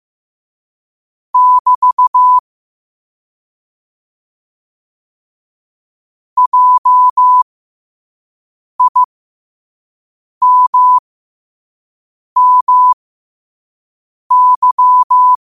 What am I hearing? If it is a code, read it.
Morse code, heard as a keyed tone, about 15 wpm, spaced as for 5 wpm, 1000 Hz. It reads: <BT> JIMMY
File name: 5wpm